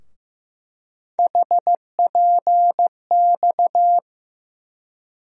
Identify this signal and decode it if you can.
Morse code, heard as a keyed tone, about 15 wpm, 700 Hz. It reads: HPX
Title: USA Call Sign Postfixes - 3 Letters 15wpm